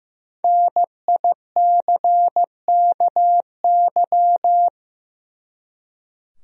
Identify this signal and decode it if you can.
Morse code, heard as a keyed tone, about 16 wpm, 700 Hz. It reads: NICKY
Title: USA Common Names - 15wpm